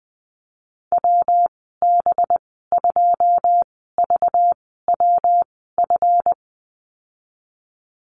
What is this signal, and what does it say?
Morse code, heard as a keyed tone, about 20 wpm, spaced as for 15 wpm, 700 Hz. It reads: WB2VWF
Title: 238 Lesson 38g - Letter q - Call Signs 20wpm